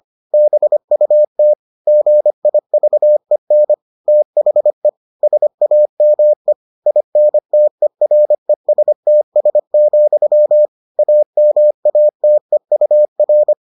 Morse code, heard as a keyed tone, about 25 wpm, 600 Hz. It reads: BUT GIVEN THE SAME INTERESTS, AMATEUR